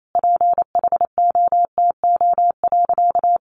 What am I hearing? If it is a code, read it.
Morse code, heard as a keyed tone, about 28 wpm, 700 Hz. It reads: PHOTO.
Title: Sentences from Top 1200 Words 28wpm